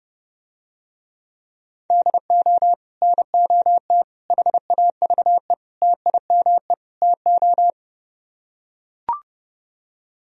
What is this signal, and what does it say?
Morse code, heard as a keyed tone, about 32 wpm, 700 Hz. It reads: DO NOT HAVE TIME TO